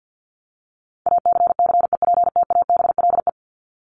Morse code, heard as a keyed tone, about 48 wpm, 700 Hz. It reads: ACCEPTABLE
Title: Top 1501-1750 Words 50wpm